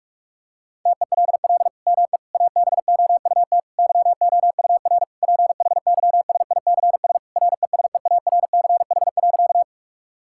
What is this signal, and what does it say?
Morse code, heard as a keyed tone, about 41 wpm, 700 Hz. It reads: TELL ME ABOUT YOUR PHYSICS RESEARCH.
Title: Sentences from Top 1000 Words - Review Part 5 - 45wpm